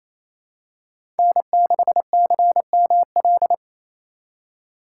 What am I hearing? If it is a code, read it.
Morse code, heard as a keyed tone, about 29 wpm, 700 Hz. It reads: N6CML
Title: USA Call Signs - 5 Characters - 2x Character Spacing - 28wpm